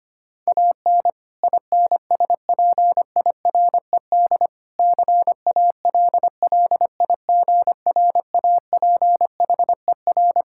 Morse code, heard as a keyed tone, about 25 wpm, 700 Hz. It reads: AN INSPIRED CALLIGRAPHER
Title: Pangram Challenge v2 - 25wpm